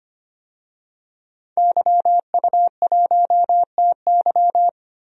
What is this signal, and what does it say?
Morse code, heard as a keyed tone, about 25 wpm, 700 Hz. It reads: YU1TY
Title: DXCC-100 Call Signs - 5 Characters - Repeated 2x - 25wpm